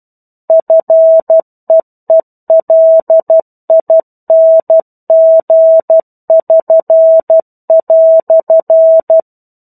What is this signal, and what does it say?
Morse code, heard as a keyed tone, about 12 wpm, 650 Hz. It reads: FEELING<SN>"
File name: 01-Chapter-01 12 WPM